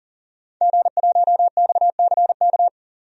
Morse code, heard as a keyed tone, about 40 wpm, 700 Hz. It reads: G1XCK
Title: DXCC-100 Call Signs - 5 Characters - 40wpm